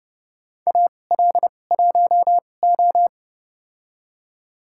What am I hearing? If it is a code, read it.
Morse code, heard as a keyed tone, about 29 wpm, spaced as for 21 wpm, 700 Hz. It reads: AL1O
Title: USA Call Signs - 4 Characters - 2x Character Spacing - 30wpm